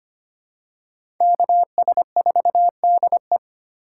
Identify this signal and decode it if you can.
Morse code, heard as a keyed tone, about 25 wpm, 700 Hz. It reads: KS4DE
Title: USA Call Signs - 5 Characters - 3x Character Spacing - 25wpm